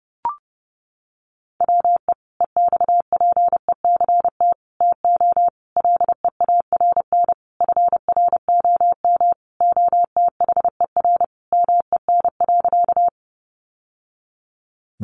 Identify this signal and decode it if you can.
Morse code, heard as a keyed tone, about 30 wpm, 700 Hz. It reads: WE EXPECT TO LEARN FROM OTHER MEN.
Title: Sentences from Top 1100 Words 30wpm